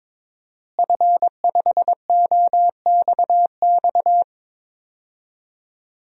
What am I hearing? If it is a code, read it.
Morse code, heard as a keyed tone, about 22 wpm, 700 Hz. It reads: F5OXX
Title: DXCC-100 Call Signs - 5 Characters - Repeated 2x - 22wpm